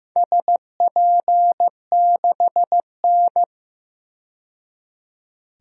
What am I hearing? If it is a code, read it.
Morse code, heard as a keyed tone, about 15 wpm, 700 Hz. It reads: SP6N
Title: DXCC-100 Call Signs - 4 Characters - 15wpm